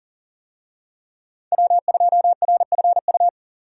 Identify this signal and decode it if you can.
Morse code, heard as a keyed tone, about 40 wpm, 700 Hz. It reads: W2RFU
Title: USA Call Signs - 5 Characters - 40wpm